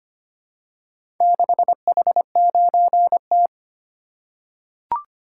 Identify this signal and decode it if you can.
Morse code, heard as a keyed tone, about 25 wpm, 700 Hz. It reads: 6H9T
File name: practice / 25wpm / Four Letter-Number Combinations - 25wpm